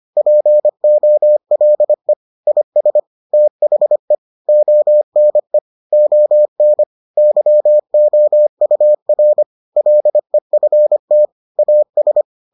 Morse code, heard as a keyed tone, about 25 wpm, 600 Hz. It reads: POLE IS THE ONE ON YOUR LEFT AS